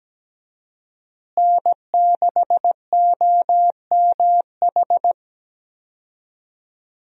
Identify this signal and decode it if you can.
Morse code, heard as a keyed tone, about 17 wpm, 700 Hz. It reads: N6OMH
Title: USA Call Signs - 5 Characters - Repeated 3x - 17wpm